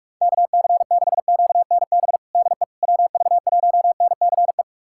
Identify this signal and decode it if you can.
Morse code, heard as a keyed tone, about 46 wpm, 690 Hz. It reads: KC6QNB DE WV1NCE
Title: USA Callsign DE Callsign - 6 Characters - 45wpm